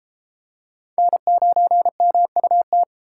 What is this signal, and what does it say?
Morse code, heard as a keyed tone, about 27 wpm, 700 Hz. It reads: N9MUT
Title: Speed Racing - USA 1x3 Call Signs 22wpm